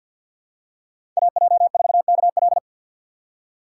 Morse code, heard as a keyed tone, about 53 wpm, 700 Hz. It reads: AJ4KL
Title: USA Call Signs - 5 Characters - 6x Character Spacing - 50wpm